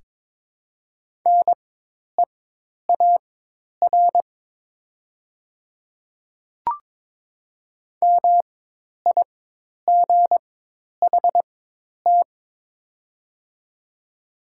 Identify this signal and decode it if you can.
Morse code, heard as a keyed tone, about 22 wpm, spaced as for 10 wpm, 700 Hz. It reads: NEAR MIGHT
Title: 4x Character Spacing - Top 500 Words - 22wpm